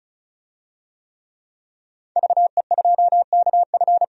Morse code, heard as a keyed tone, about 36 wpm, 700 Hz. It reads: VE2KF
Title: DXCC-100 Call Signs - 5 Characters - 35wpm